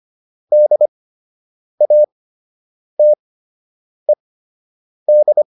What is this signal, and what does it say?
Morse code, heard as a keyed tone, about 25 wpm, spaced as for 8 wpm, 600 Hz. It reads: DATED